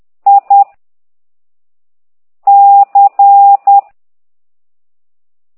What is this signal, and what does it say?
Morse code, heard as a keyed tone, about 10 wpm, 800 Hz. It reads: I C
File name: ct2idw / exerc15